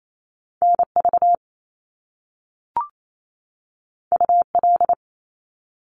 Morse code, heard as a keyed tone, about 28 wpm, 700 Hz. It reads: NV UL
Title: Top Two Letter Combinations in Words 28wpm